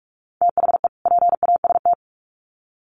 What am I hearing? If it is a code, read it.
Morse code, heard as a keyed tone, about 46 wpm, 700 Hz. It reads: THE PAST
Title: Sets of 2 Words 45wpm